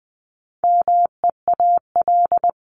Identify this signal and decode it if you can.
Morse code, heard as a keyed tone, about 20 wpm, 700 Hz. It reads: MEAL
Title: Top 1751-2000 Words 20wpm